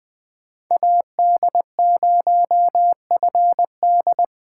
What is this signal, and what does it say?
Morse code, heard as a keyed tone, about 20 wpm, 700 Hz. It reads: AD0FD
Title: USA Call Signs - 5 Characters - 4x Character Spacing - 20wpm